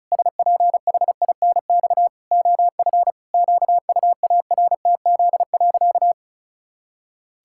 Morse code, heard as a keyed tone, about 36 wpm, 700 Hz. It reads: SPHINX OF QUARTZ.
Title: Pangram Challenge v2 - 35wpm